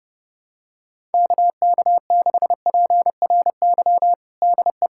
Morse code, heard as a keyed tone, about 30 wpm, 700 Hz. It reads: KK6PRY DE
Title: USA Callsign DE Callsign - 6 Characters - 30wpm